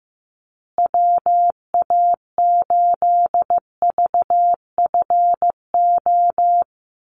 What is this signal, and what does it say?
Morse code, heard as a keyed tone, about 15 wpm, 700 Hz. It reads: WA8VFO
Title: USA Call Signs Repeated 3x 15wpm